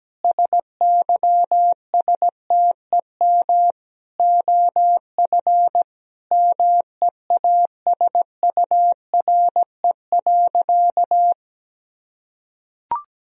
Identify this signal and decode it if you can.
Morse code, heard as a keyed tone, about 17 wpm, 700 Hz. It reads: SYSTEM OF MEASURE.
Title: Sentences from Top 1000 Words - Review Part 3 - 17wpm